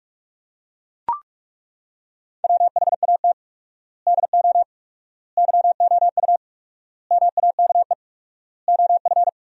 Morse code, heard as a keyed tone, about 42 wpm, 700 Hz. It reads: WHAT DO YOU MAKE OF
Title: 4x Word-Spacing - Sentences from Top 100 Words - 45wpm